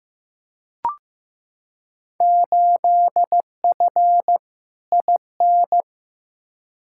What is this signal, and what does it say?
Morse code, heard as a keyed tone, about 15 wpm, 700 Hz. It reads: 8F IN
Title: Field Day Exchanges Encore - 15wpm